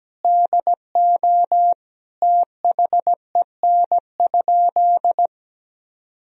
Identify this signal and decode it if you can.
Morse code, heard as a keyed tone, about 17 wpm, 700 Hz. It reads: DO THEN?
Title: Sentences from Top 100 Words and Abbreviations 17wpm